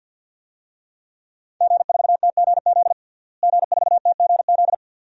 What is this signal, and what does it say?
Morse code, heard as a keyed tone, about 50 wpm, 700 Hz. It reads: G4TGZ G4TGZ
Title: DXCC-100 Call Signs - 5 Characters - Repeated 2x - 50wpm